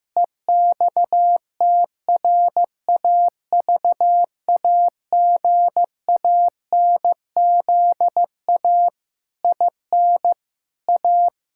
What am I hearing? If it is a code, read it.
Morse code, heard as a keyed tone, about 16 wpm, 700 Hz. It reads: EXTRAVAGANZA IN A
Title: Pangram Challenge v2 - 15wpm